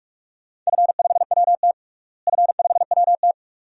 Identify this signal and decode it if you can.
Morse code, heard as a keyed tone, about 46 wpm, 700 Hz. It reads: F5WT F5WT
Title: DXCC-100 Call Signs - 4 Characters - Repeated 2x - 45wpm